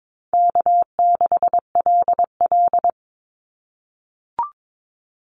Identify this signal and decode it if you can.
Morse code, heard as a keyed tone, about 22 wpm, 700 Hz. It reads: K6LL
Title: Call Signs - 22wpm